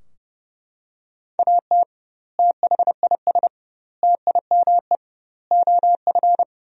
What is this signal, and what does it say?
Morse code, heard as a keyed tone, about 30 wpm, 700 Hz. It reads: AT THIS TIME OF
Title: 2x Word-Spacing - Sentences from Top 300 Words - 30wpm